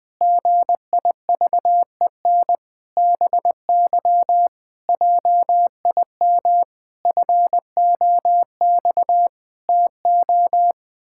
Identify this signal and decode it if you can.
Morse code, heard as a keyed tone, about 20 wpm, 700 Hz. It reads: GIVEN BY JIM FOX TO
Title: Pangram Challenge v2 - 20wpm